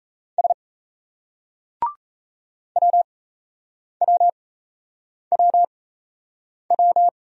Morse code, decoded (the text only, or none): S WWWW